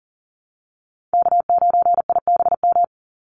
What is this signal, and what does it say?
Morse code, heard as a keyed tone, about 40 wpm, 700 Hz. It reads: K9IBM
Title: Call Signs Repeated 2x - 40wpm